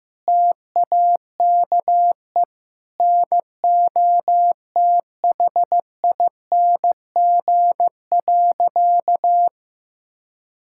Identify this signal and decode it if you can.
Morse code, heard as a keyed tone, about 15 wpm, 700 Hz. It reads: TAKE NOTHING.
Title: Sentences from Top 1000 Words - Review Part 6 - 15wpm